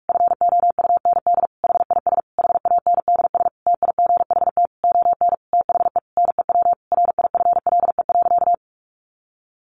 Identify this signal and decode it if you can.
Morse code, heard as a keyed tone, about 44 wpm, 700 Hz. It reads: FOUND HIS HANDS TIGHT ON THE NEW RIFLE.
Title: A Sound Of Thunder 45wpm